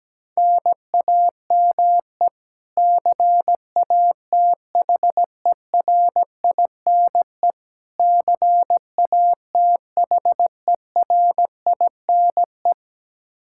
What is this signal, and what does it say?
Morse code, heard as a keyed tone, about 17 wpm, 700 Hz. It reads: NAME CATHERINE CATHERINE
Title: Practice QSO - Exchange 1 - Beginning - 17wpm